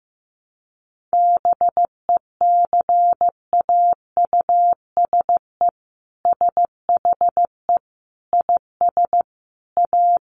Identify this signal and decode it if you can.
Morse code, heard as a keyed tone, about 15 wpm, 700 Hz. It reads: BECAUSE SHE IS A
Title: Sentences from Top 1100 Words 15wpm